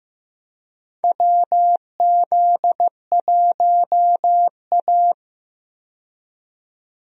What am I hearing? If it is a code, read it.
Morse code, heard as a keyed tone, about 15 wpm, 700 Hz. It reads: WZ1A
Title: USA Call Signs - 4 Characters - 15wpm